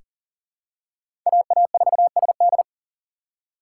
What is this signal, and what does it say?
Morse code, heard as a keyed tone, about 40 wpm, 700 Hz. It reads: AA4SD
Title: USA Call Signs - 5 Characters - 6x Character Spacing - 40wpm